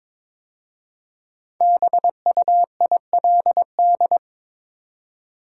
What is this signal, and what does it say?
Morse code, heard as a keyed tone, about 22 wpm, 700 Hz. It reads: BUILD
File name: 5 Letter Words - First 3 Letters Given - 22wpm